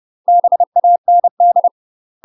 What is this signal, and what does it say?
Morse code, heard as a keyed tone, about 30 wpm, 700 Hz. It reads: BAND
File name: UK MASTER QSOs TRAINING / 30WPM / UK MASTER QSOs 0018